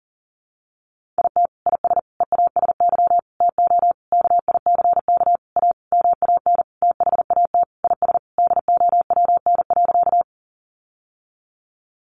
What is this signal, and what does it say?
Morse code, heard as a keyed tone, about 40 wpm, 700 Hz. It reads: IT IS EASY TO KICK A MAN THAT IS DOWN.
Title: Sentences from Top 1300 Words 40wpm